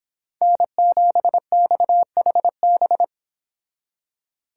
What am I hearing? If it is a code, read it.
Morse code, heard as a keyed tone, about 28 wpm, 700 Hz. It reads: N7XHB